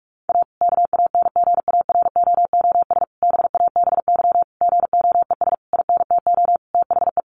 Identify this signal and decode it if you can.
Morse code, heard as a keyed tone, about 44 wpm, 700 Hz. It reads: A KANGAROOS BABY GOES INTO THE
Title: Easy Phrases 45wpm